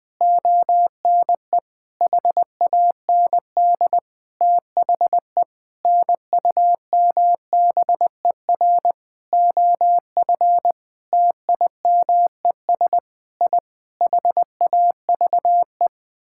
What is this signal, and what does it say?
Morse code, heard as a keyed tone, about 20 wpm, 700 Hz. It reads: ONE HAND THE NUMBER OF TIMES I HAVE
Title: Sentences with 8-Word Phrases - 20wpm